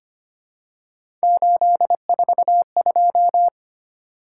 Morse code, heard as a keyed tone, about 25 wpm, 700 Hz. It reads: 842
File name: Three Letter-Number Combinations - 25wpm